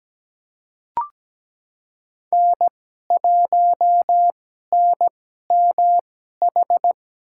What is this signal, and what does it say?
Morse code, heard as a keyed tone, about 17 wpm, spaced as for 12 wpm, 700 Hz. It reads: N1NMH
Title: USA Call Signs - 5 Characters - 2x Character Spacing - 17wpm